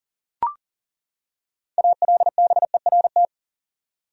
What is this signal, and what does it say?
Morse code, heard as a keyed tone, about 40 wpm, 700 Hz. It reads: ALBERT
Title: USA Common Names - 40wpm